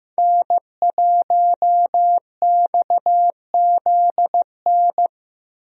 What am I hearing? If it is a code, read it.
Morse code, heard as a keyed tone, about 15 wpm, 700 Hz. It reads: N1XZN
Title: USA Call Signs - 5 Characters - Repeated 2x - 15wpm